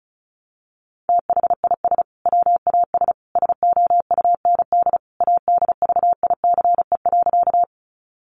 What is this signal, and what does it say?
Morse code, heard as a keyed tone, about 34 wpm, 700 Hz. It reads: THIS WAS SOUND ADVICE.
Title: Sentences from Top 1100 Words 35wpm